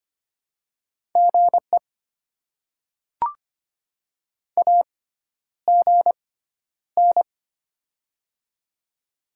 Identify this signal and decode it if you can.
Morse code, heard as a keyed tone, about 25 wpm, spaced as for 11 wpm, 700 Hz. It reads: GE AGN
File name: Top 20 QSO Elements - 6x Character Spacing - 25wpm